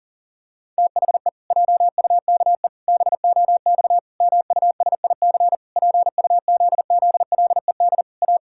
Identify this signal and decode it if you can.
Morse code, heard as a keyed tone, about 40 wpm, 700 Hz. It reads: THE JUKE BOX MUSIC PUZZLED A
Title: Pangram Challenge v2 - 40wpm